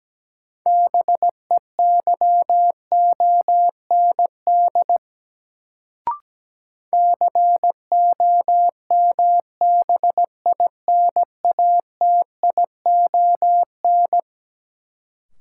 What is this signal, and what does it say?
Morse code, heard as a keyed tone, about 17 wpm, 700 Hz. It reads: BEYOND COMBINATION
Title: Sentences from Top 1000 Words - Review Part 6 - New Word Drill - 17wpm